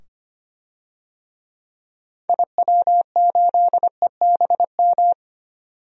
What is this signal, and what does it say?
Morse code, heard as a keyed tone, about 25 wpm, 700 Hz. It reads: IW8EBM
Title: DXCC-100 Call Signs - 6 Characters - 25wpm